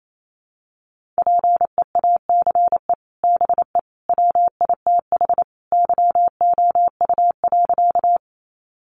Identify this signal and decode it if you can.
Morse code, heard as a keyed tone, about 28 wpm, 700 Hz. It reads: PEACE BE WITH YOU.
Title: Sentences from Top 1500 Words 28wpm